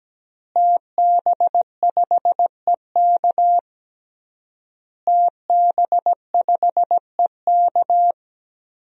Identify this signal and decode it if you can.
Morse code, heard as a keyed tone, about 17 wpm, 700 Hz. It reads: TB5EK TB5EK
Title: DXCC-100 Call Signs - 5 Characters - Repeated 3x - 17wpm